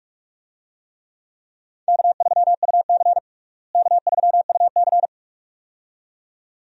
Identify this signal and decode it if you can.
Morse code, heard as a keyed tone, about 43 wpm, 700 Hz. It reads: K3UC K3UC
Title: USA Call Signs - 4 Characters - Repeated 2x - 45wpm